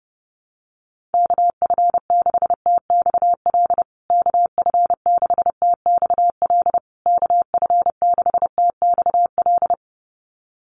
Morse code, heard as a keyed tone, about 30 wpm, 700 Hz. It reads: KF6TXL KF6TXL KF6TXL
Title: USA Call Signs Repeated 3x 30wpm